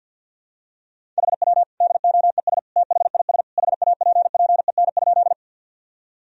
Morse code, heard as a keyed tone, about 50 wpm, 700 Hz. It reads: HW DOES THIS HAPPEN?
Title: Sentences from Top 400 Words and Abbreviations 50wpm